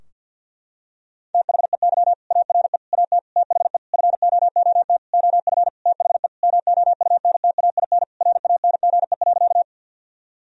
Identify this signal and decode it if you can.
Morse code, heard as a keyed tone, about 50 wpm, 700 Hz. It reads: THEY ARE AT THE FOOT OF THE MOUNTAIN RANGE.
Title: Sentences from Top 1000 Words - Review Part 4 - 50wpm